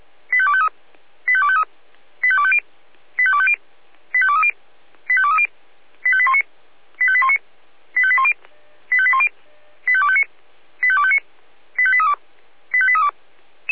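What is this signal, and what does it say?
Five-tone selective call (ZVEI), heard as quick runs of five tones, tones of about 70 ms. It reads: 85253 85253 85250 85250 85230 85230 85819 85819 85810 85810 85259 85259 85832 85832
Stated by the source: ZVEI five-tone selective call code 85100, received by SDR.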